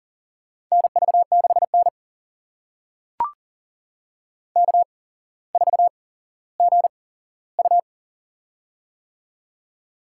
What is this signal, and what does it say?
Morse code, heard as a keyed tone, about 40 wpm, spaced as for 17 wpm, 700 Hz. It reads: NV6N K4GU
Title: USA Call Signs - 4 Characters - 8x Character Spacing - 40wpm